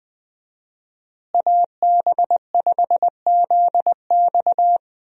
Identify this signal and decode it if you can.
Morse code, heard as a keyed tone, about 20 wpm, 700 Hz. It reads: AB5ZX
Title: USA Call Signs - 5 Characters - Repeated 3x - 20wpm